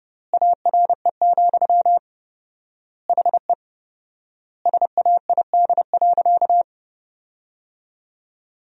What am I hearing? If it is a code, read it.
Morse code, heard as a keyed tone, about 30 wpm, 700 Hz. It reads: ARE, HE SAID.
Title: 4x Word-Spacing - Sentences from Top 100 Words - 30wpm